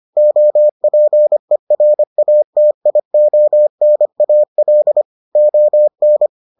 Morse code, heard as a keyed tone, about 25 wpm, 600 Hz. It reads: OPERATIONAL ON